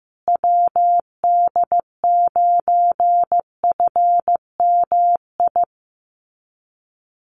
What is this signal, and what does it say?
Morse code, heard as a keyed tone, about 15 wpm, 700 Hz. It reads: WD9FMI
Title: USA Call Signs Repeated 3x 15wpm